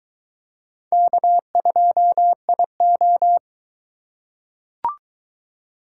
Morse code, heard as a keyed tone, about 22 wpm, 700 Hz. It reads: K2IO